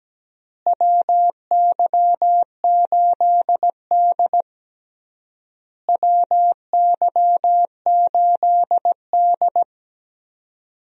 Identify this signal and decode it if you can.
Morse code, heard as a keyed tone, about 17 wpm, 700 Hz. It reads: WY8D WY8D
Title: USA Call Signs - 4 Characters - Repeated 3x - 17wpm